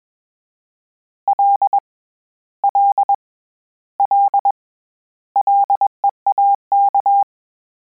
Morse code, heard as a keyed tone, about 21 wpm, 800 Hz. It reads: L L L LEAK
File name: Combinations / Combinations-mp3s / Combinations@20